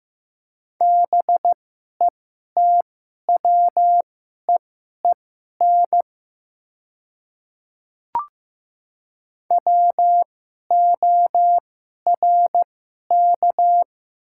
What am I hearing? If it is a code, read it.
Morse code, heard as a keyed tone, about 15 wpm, spaced as for 11 wpm, 700 Hz. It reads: BETWEEN WORK
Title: 2x Character Spacing - Top 300 Words - 15wpm